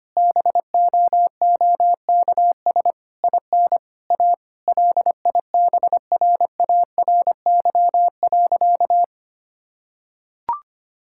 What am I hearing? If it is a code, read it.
Morse code, heard as a keyed tone, about 25 wpm, 700 Hz. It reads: BOOKS IN A LIBRARY.
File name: practice / 25wpm / Difficult Sentences with Hints - 25wpm